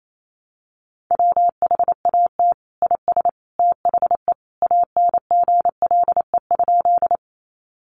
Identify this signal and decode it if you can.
Morse code, heard as a keyed tone, about 28 wpm, 700 Hz. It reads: WHAT IS THE ANGLE?
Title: Sentences from Top 1500 Words 28wpm